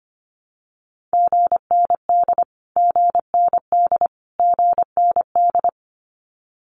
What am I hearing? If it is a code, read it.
Morse code, heard as a keyed tone, about 25 wpm, 700 Hz. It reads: GND GND GND
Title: Abbreviations Repeated 3x - 25wpm